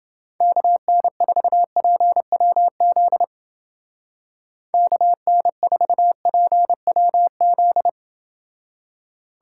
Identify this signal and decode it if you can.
Morse code, heard as a keyed tone, about 30 wpm, 700 Hz. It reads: KN4PWZ KN4PWZ